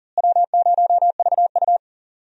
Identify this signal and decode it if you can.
Morse code, heard as a keyed tone, about 40 wpm, 700 Hz. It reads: W0VU
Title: USA Call Signs - 4 Characters - Repeated 2x - 40wpm